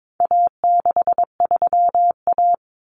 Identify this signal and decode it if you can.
Morse code, heard as a keyed tone, about 22 wpm, 700 Hz. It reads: A63A
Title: Common Cut Numbers - 4-Digit 22wpm